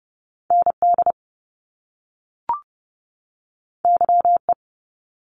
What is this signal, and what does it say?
Morse code, heard as a keyed tone, about 30 wpm, 700 Hz. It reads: ND YE
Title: Top Two Letter Combinations in Words 30wpm